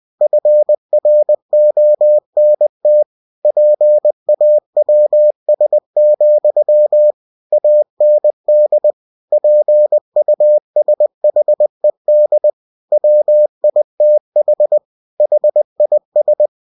Morse code, heard as a keyed tone, about 20 wpm, 600 Hz. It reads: FRONT PAWS, AND PUSHED WITH HIS
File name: cw-books / winnie-the-pooh / winnie-20wpm-page-0016